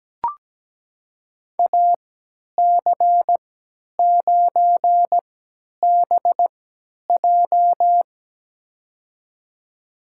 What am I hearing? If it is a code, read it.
Morse code, heard as a keyed tone, about 17 wpm, spaced as for 10 wpm, 700 Hz. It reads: AC9BJ
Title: USA Call Signs - 5 Characters - 3x Character Spacing - 17wpm